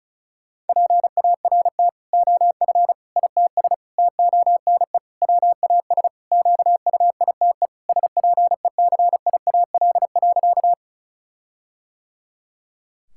Morse code, heard as a keyed tone, about 34 wpm, 700 Hz. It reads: PART OF ITS TONE WAS QUITE SPECIAL.
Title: Sentences from Top 1600 Words 35wpm